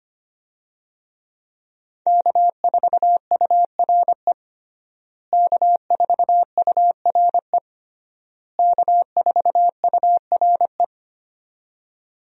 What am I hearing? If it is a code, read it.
Morse code, heard as a keyed tone, about 25 wpm, 700 Hz. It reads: K4URE K4URE K4URE
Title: USA Call Signs - 5 Characters - Repeated 3x - 25wpm